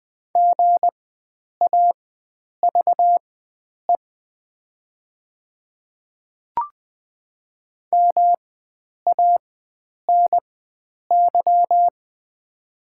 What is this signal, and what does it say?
Morse code, heard as a keyed tone, about 21 wpm, spaced as for 9 wpm, 700 Hz. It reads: GAVE MANY
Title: 4x Character Spacing - Top 500 Words - 20wpm